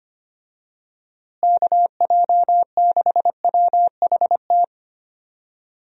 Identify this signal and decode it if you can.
Morse code, heard as a keyed tone, about 25 wpm, 700 Hz. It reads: KJ6WHT
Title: USA Call Signs - 6 Characters - Repeated 3x - 25wpm